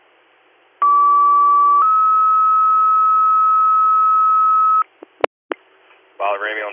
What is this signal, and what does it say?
Two-tone page, heard as a one-second tone, then a three-second tone, about 1150 then 1290 Hz, with speech. It is heard from 0.8 to 4.8 s.